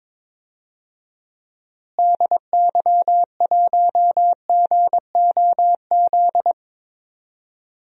Morse code, heard as a keyed tone, about 21 wpm, 700 Hz. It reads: DY1GOZ
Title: DXCC-100 Call Signs - 6 Characters - Repeated 2x - 22wpm